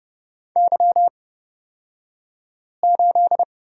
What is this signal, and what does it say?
Morse code, heard as a keyed tone, about 31 wpm, spaced as for 5 wpm, 700 Hz. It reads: Y8
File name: Lesson 36d - BT - Two Characters 30wpm